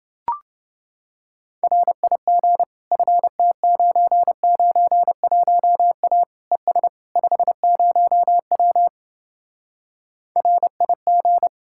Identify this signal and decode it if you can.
Morse code, heard as a keyed tone, about 29 wpm, 700 Hz. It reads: RIG FT991A ES 50W RIG